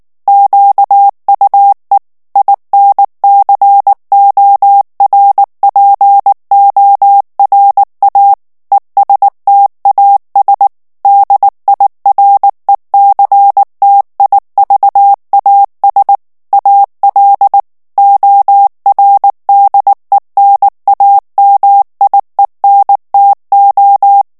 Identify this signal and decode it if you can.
Morse code, heard as a keyed tone, about 19 wpm, 800 Hz. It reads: QUE INCORPORA ESTAS DIRECTIVAS AL ORDENAMIENTO
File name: morse mp3 / ESPAGNOL TEXTE EN CLAIR 2 VITESSE 18 MOTS MINUTES